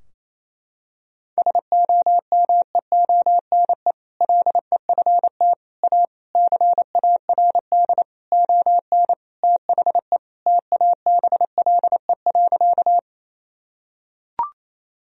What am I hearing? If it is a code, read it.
Morse code, heard as a keyed tone, about 28 wpm, 700 Hz. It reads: SOMEONE LEFT A CARD ON THE TABLE.
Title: Sentences from Top 1000 Words - Review Part 4 - 28wpm